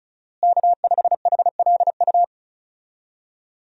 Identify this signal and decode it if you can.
Morse code, heard as a keyed tone, about 36 wpm, 700 Hz. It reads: K5HLU
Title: USA Call Signs - 5 Characters - Repeated 2x - 35wpm